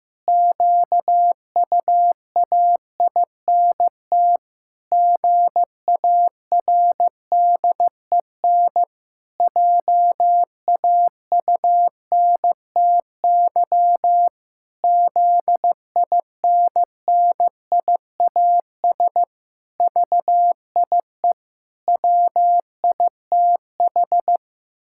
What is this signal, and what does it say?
Morse code, heard as a keyed tone, about 14 wpm, 700 Hz. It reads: QUAINT GARDEN JAUNTY ZINNIAS VIE WITH
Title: Pangram Challenge v2 - 15wpm